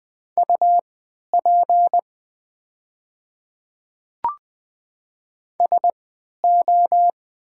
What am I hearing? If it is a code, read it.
Morse code, heard as a keyed tone, about 20 wpm, spaced as for 11 wpm, 700 Hz. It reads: UP SO
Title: 3x Character Spacing - Top 100 Words - 20wpm